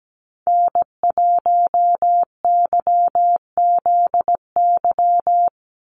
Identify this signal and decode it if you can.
Morse code, heard as a keyed tone, about 17 wpm, 700 Hz. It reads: N1YZY
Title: USA Call Signs 17wpm